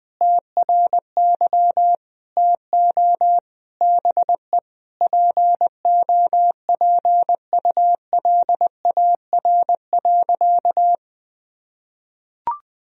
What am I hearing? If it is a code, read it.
Morse code, heard as a keyed tone, about 20 wpm, 700 Hz. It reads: TRY TO BE POPULAR.
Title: Sentences from Top 1000 Words - Review Part 3 - 20wpm